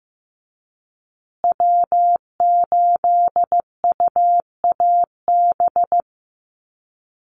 Morse code, heard as a keyed tone, about 15 wpm, 700 Hz. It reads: W8UAB
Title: USA Call Signs Repeated 2x 15wpm